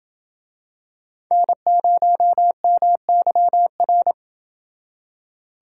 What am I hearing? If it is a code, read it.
Morse code, heard as a keyed tone, about 28 wpm, 700 Hz. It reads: N0MYR